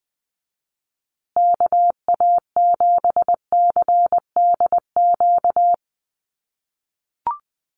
Morse code, heard as a keyed tone, about 20 wpm, 700 Hz. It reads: KA7CDQ
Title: USA Call Signs Repeated 2x 20wpm